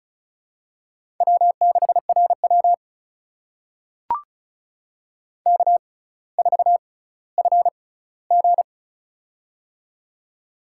Morse code, heard as a keyed tone, about 36 wpm, spaced as for 18 wpm, 700 Hz. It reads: W6RW K4FG
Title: USA Call Signs - 4 Characters - 6x Character Spacing - 35wpm